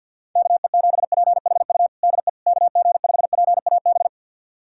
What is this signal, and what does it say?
Morse code, heard as a keyed tone, about 53 wpm, 700 Hz. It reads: KE7PHU DE KG5PAB
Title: USA Callsign DE Callsign - 6 Characters - 50wpm